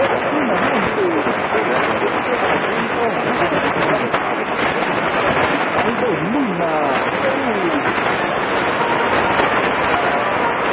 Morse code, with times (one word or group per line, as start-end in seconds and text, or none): none